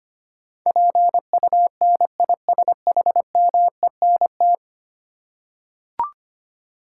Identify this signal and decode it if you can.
Morse code, heard as a keyed tone, about 25 wpm, 700 Hz. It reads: PUNISHMENT
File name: Ten Letter Words - 25wpm